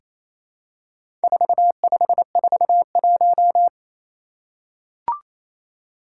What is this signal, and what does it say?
Morse code, heard as a keyed tone, about 29 wpm, 700 Hz. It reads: 4541
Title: Four Challenging Letter-Number Combinations 28wpm